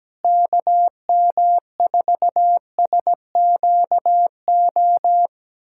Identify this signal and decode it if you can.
Morse code, heard as a keyed tone, about 17 wpm, 700 Hz. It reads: KM4SQO
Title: USA Callsign DE Callsign 17wpm